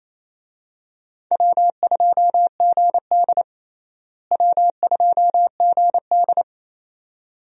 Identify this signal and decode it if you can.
Morse code, heard as a keyed tone, about 28 wpm, 700 Hz. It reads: W2GD W2GD
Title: USA Call Signs - 4 Characters - Repeated 2x - 28wpm